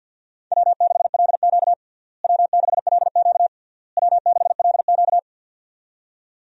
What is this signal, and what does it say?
Morse code, heard as a keyed tone, about 50 wpm, 700 Hz. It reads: W6LQ W6LQ W6LQ
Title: USA Call Signs - 4 Characters - Repeated 3x - 50wpm